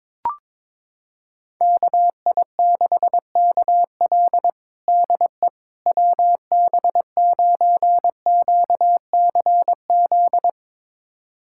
Morse code, heard as a keyed tone, about 22 wpm, 700 Hz. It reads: KI6KL DE WB9QCZ
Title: USA Callsign DE Callsign 22wpm